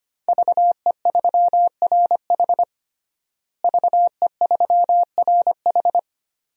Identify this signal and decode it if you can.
Morse code, heard as a keyed tone, about 25 wpm, 700 Hz. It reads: VE3RH VE3RH
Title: DXCC-100 Call Signs - 5 Characters - Repeated 2x - 25wpm